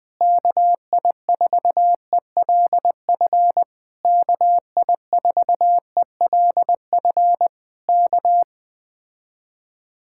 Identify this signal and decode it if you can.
Morse code, heard as a keyed tone, about 20 wpm, 700 Hz. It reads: KI4ELF KI4ELF K
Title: Speed-Racing Common QSO Phrases Encore - 17wpm